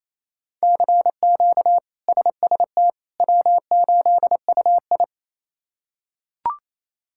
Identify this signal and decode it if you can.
Morse code, heard as a keyed tone, about 27 wpm, 700 Hz. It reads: CQ SST W8UI
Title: K1USN SST QSO Practice - 28wpm